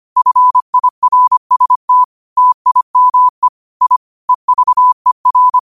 Morse code, heard as a keyed tone, about 25 wpm, 1000 Hz. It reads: FIRST TIME I EVER